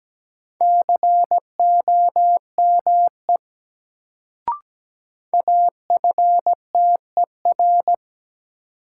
Morse code, heard as a keyed tone, about 17 wpm, 700 Hz. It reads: COME AFTER
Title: Sets of 2 Words - New Word Drill - 17wpm